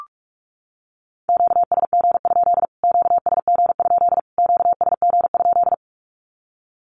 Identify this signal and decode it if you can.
Morse code, heard as a keyed tone, about 44 wpm, 700 Hz. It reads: QSG? QSG? QSG?
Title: Q-Codes Repeated 3x 45wpm